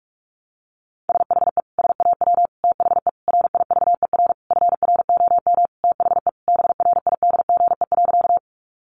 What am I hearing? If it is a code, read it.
Morse code, heard as a keyed tone, about 44 wpm, 700 Hz. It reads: SHE SAW THE RIVER FROM THE BRIDGE.
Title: Sentences from Top 1200 Words 45wpm